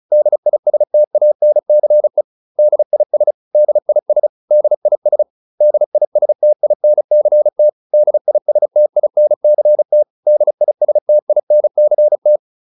Morse code, heard as a keyed tone, about 34 wpm, 600 Hz. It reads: DISTANCE DIS DIS DIS DISTINCT DISTINCT DISTINCT